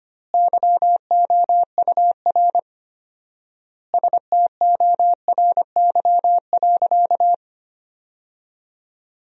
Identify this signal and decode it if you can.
Morse code, heard as a keyed tone, about 25 wpm, 700 Hz. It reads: YOUR STORY.
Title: 4x Word-Spacing - Sentences from Top 300 Words - 25wpm